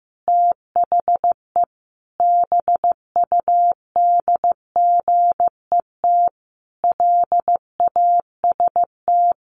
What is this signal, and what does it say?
Morse code, heard as a keyed tone, about 15 wpm, 700 Hz. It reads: THE BUDGET LAST
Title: Sentences from Top 1200 Words 15wpm